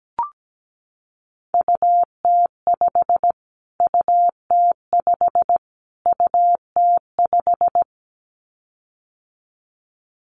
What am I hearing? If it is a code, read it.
Morse code, heard as a keyed tone, about 17 wpm, 700 Hz. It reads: UT5 UT5 UT5
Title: Callsign Prefixes for Top 25 DXCC - Repeated 3x - 17wpm